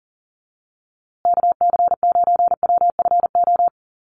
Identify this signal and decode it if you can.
Morse code, heard as a keyed tone, about 40 wpm, 700 Hz. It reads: KC9WFO
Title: USA Call Signs Repeated 3x 40wpm